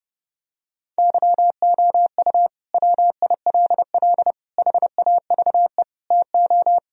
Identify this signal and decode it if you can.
Morse code, heard as a keyed tone, about 30 wpm, 700 Hz. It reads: YOU WILL HAVE TO
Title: Sentences from Top 1000 Words - Review Part 1 - 30wpm